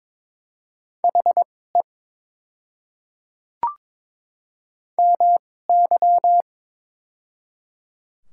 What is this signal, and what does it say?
Morse code, heard as a keyed tone, about 22 wpm, spaced as for 16 wpm, 700 Hz. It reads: HE MY
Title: 2x Character Spacing - Top 300 Words - 22wpm